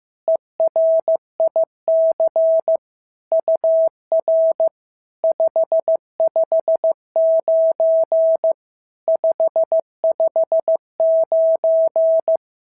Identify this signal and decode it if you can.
Morse code, heard as a keyed tone, about 15 wpm, 650 Hz. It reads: ERIC UR 559 559
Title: SOTA QSO Practice 15wpm